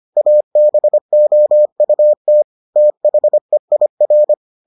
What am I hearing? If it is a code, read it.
Morse code, heard as a keyed tone, about 25 wpm, 600 Hz. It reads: ABOUT THEIR